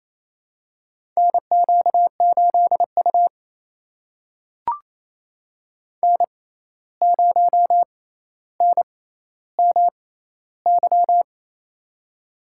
Morse code, heard as a keyed tone, about 28 wpm, spaced as for 13 wpm, 700 Hz. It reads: NQ8U N0NMY
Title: USA Call Signs - 6x Character Spacing - 28wpm